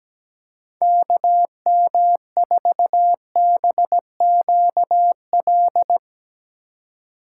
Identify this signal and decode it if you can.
Morse code, heard as a keyed tone, about 18 wpm, 700 Hz. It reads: KM4BQL
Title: Speed Racing - USA 2x3 Call Signs 17wpm